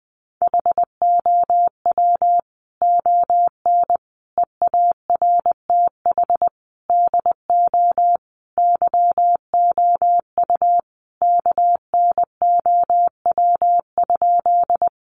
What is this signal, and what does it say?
Morse code, heard as a keyed tone, about 20 wpm, 700 Hz. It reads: HOW ON EARTH DO YOU KNOW?
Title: Sentences from Top 200 Words 20wpm